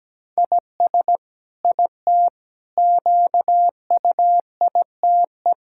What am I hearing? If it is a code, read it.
Morse code, heard as a keyed tone, about 17 wpm, 700 Hz. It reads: IS IT QUITE
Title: Sentences from Top 1000 Words - Review Part 6 - 17wpm